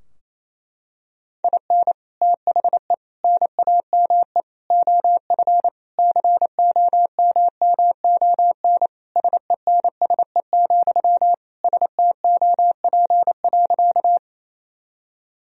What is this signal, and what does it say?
Morse code, heard as a keyed tone, about 28 wpm, 700 Hz. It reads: IN THE NAME OF COMMON SENSE, STOP.
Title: Sentences from Top 1000 Words - Review Part 4 - 28wpm